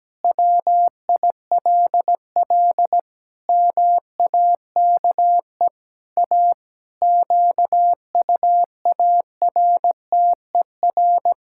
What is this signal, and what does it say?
Morse code, heard as a keyed tone, about 17 wpm, 700 Hz. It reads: WILL MAKE A QUARTER
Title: Pangram Challenge v2 - 17wpm